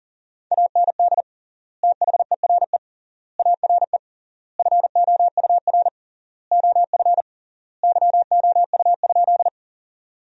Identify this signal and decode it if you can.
Morse code, heard as a keyed tone, about 40 wpm, 700 Hz. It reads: AND THERE ARE FOUR OF YOU?
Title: 3x Word-Spacing - Sentences from Top 300 Words - 40wpm